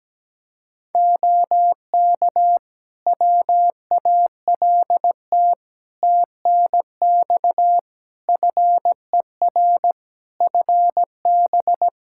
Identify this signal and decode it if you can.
Morse code, heard as a keyed tone, about 17 wpm, 700 Hz. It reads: OK WALT TNX FER FB
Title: Practice QSO - Exchange 4 - Ending - 17wpm